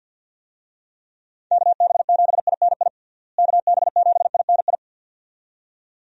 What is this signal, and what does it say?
Morse code, heard as a keyed tone, about 50 wpm, 700 Hz. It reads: KB7INI KB7INI
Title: USA Call Signs - 6 Characters - Repeated 2x - 50wpm